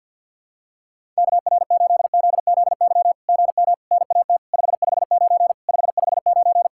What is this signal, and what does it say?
Morse code, heard as a keyed tone, about 50 wpm, 700 Hz. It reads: KR8ZZY GM NAT 559 559
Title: SOTA QSO Practice 50wpm